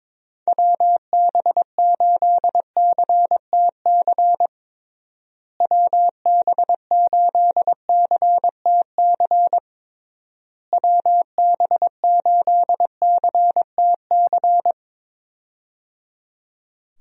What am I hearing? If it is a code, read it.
Morse code, heard as a keyed tone, about 22 wpm, 700 Hz. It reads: WB8CTC WB8CTC WB8CTC
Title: USA Call Signs - 6 Characters - Repeated 3x - 22wpm